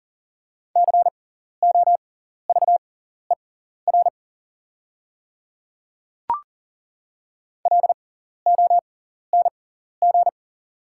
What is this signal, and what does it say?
Morse code, heard as a keyed tone, about 39 wpm, spaced as for 14 wpm, 700 Hz. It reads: COVER LONG
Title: 6x Character Spacing - Top 300 Words - 40wpm